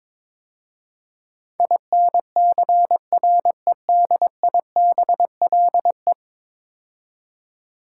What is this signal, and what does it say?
Morse code, heard as a keyed tone, about 22 wpm, 700 Hz. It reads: INCREDIBLE
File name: practice / 22wpm / 10 Letter Words - First 3 Letters Given - 22wpm